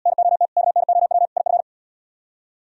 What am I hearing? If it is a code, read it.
Morse code, heard as a keyed tone, about 72 wpm, 700 Hz. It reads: NOT KNOW IF